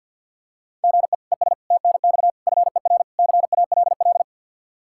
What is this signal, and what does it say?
Morse code, heard as a keyed tone, about 49 wpm, 700 Hz. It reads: GE ES TNX FER CALL
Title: Practice QSO - Exchange 1 - Beginning - 50wpm